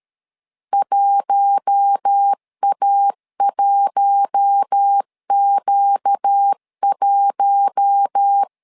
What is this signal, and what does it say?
Morse code, heard as a keyed tone, about 13 wpm, 800 Hz. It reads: 1A1Q1
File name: LEZIONI / 4 Q1 12wpm 2028